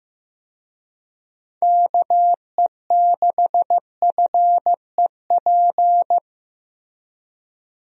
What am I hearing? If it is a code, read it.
Morse code, heard as a keyed tone, about 15 wpm, 700 Hz. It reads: KE6FEP
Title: USA Call Signs - 6 Characters - Repeated 2x - 15wpm